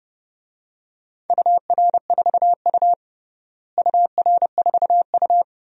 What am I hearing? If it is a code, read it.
Morse code, heard as a keyed tone, about 30 wpm, 700 Hz. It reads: UR4U UR4U
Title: DXCC-100 Call Signs - 4 Characters - Repeated 2x - 30wpm